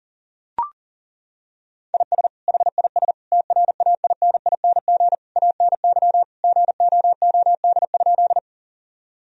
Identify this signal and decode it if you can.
Morse code, heard as a keyed tone, about 37 wpm, 700 Hz. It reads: IS HIS TRAINING ANY GOOD?
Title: Sentences from Top 1000 Words - Review Part 4 - 40wpm